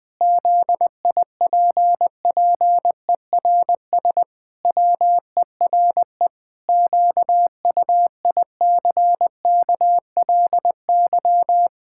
Morse code, heard as a keyed tone, about 21 wpm, 700 Hz. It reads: ZIPPERS WERE QUICKLY
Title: Pangram Challenge v2 - 20wpm